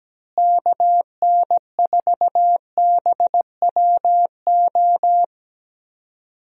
Morse code, heard as a keyed tone, about 17 wpm, 700 Hz. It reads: KN4BWO
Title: USA Call Signs - 6 Characters - Repeated 2x - 17wpm